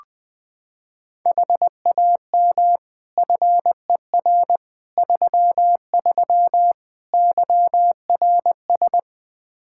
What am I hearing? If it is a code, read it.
Morse code, heard as a keyed tone, about 21 wpm, 700 Hz. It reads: HAM FER 33 YRS
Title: Practice QSO - Exchange 3 - More Info - 20wpm